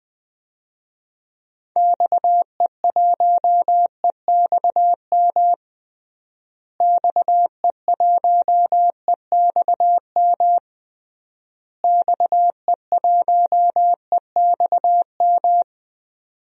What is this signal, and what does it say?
Morse code, heard as a keyed tone, about 20 wpm, 700 Hz. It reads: XE1EXM XE1EXM XE1EXM
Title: DXCC-100 Call Signs - 6 Characters - Repeated 3x - 20wpm